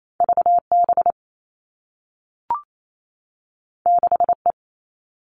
Morse code, heard as a keyed tone, about 28 wpm, 700 Hz. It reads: VB 6E
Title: All Cut Numbers - 2-Digit 28wpm